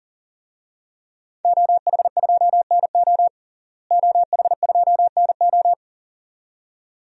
Morse code, heard as a keyed tone, about 40 wpm, 700 Hz. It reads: OH2NO OH2NO
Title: DXCC-100 Call Signs - 5 Characters - Repeated 2x - 40wpm